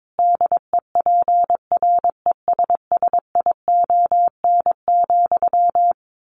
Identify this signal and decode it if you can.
Morse code, heard as a keyed tone, about 22 wpm, 700 Hz. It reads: DEPRESSION,
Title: A Sound Of Thunder 22wpm